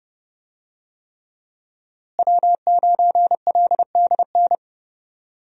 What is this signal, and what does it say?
Morse code, heard as a keyed tone, about 29 wpm, 700 Hz. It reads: W9LDN
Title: USA Call Signs - 5 Characters - Repeated 3x - 30wpm